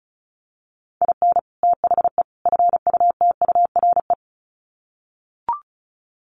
Morse code, heard as a keyed tone, about 34 wpm, 700 Hz. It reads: IN THE FUTURE
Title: Sets of 3 Words 35wpm